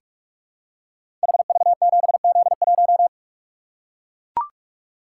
Morse code, heard as a keyed tone, about 44 wpm, 700 Hz. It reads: HV7Z1
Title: Code Groups - 5 Letter:Numbers - 45wpm